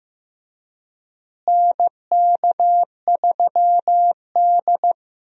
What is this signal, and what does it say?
Morse code, heard as a keyed tone, about 15 wpm, 700 Hz. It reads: NK3D
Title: USA Call Signs - 4 Characters - Repeated 2x - 15wpm